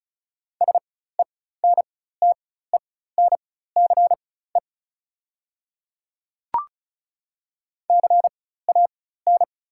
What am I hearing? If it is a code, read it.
Morse code, heard as a keyed tone, about 34 wpm, spaced as for 16 wpm, 690 Hz. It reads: SENTENCE CAN
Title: 4x Character Spacing - Top 200 Words - 35wpm